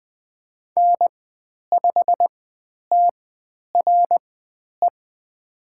Morse code, heard as a keyed tone, about 20 wpm, spaced as for 10 wpm, 700 Hz. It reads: N5TRE
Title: 109 Lesson 18g - Letter m - Call Signs 20wpm